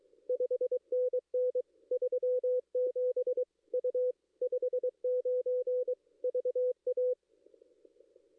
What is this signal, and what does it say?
Morse code, heard as a keyed tone, about 23 wpm, 470 Hz. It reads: 5NN 37 U 59 VA
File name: contests / ss-cw-k5tr-op-n5rz -2300